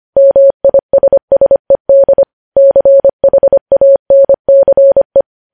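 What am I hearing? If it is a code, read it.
Morse code, heard as a keyed tone, about 25 wpm, 550 Hz. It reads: MISSED CHANCE
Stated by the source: Morse news headlines at 25 WPM.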